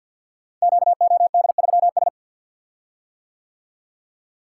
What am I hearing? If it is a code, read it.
Morse code, heard as a keyed tone, about 45 wpm, 700 Hz. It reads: QOD3S
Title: Code Groups - 5 Letter:Numbers - 50wpm